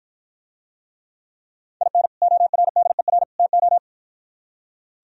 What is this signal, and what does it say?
Morse code, heard as a keyed tone, about 52 wpm, 700 Hz. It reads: IN ORDER TO